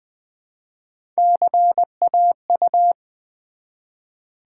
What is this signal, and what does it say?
Morse code, heard as a keyed tone, about 20 wpm, 700 Hz. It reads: CAU
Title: Build A Word - 1 Letter At A Time - 5-Letter Words - 20wpm